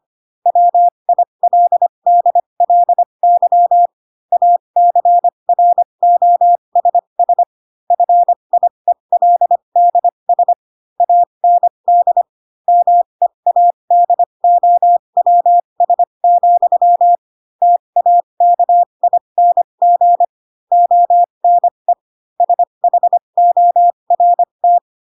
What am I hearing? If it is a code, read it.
Morse code, heard as a keyed tone, about 25 wpm, spaced as for 22 wpm, 700 Hz. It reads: WILDLY ACROSS FIELDS AND MEADOWS, TAKING ONE SHORT
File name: Pinocchio / Pinocchio ch4 18 25 ik1hge